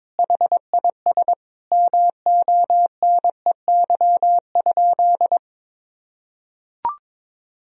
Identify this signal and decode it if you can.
Morse code, heard as a keyed tone, about 22 wpm, 700 Hz. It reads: HIS MONEY?
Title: Sentences from Top 400 Words and Abbreviations 22wpm